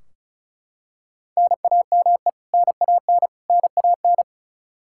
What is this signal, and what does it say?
Morse code, heard as a keyed tone, about 35 wpm, 700 Hz. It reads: NAME NAN NAN
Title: K1USN SST QSO Practice 35wpm